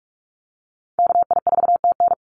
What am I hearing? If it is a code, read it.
Morse code, heard as a keyed tone, about 44 wpm, 700 Hz. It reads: KI4TN
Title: NA SOTA Activator Callsigns 45wpm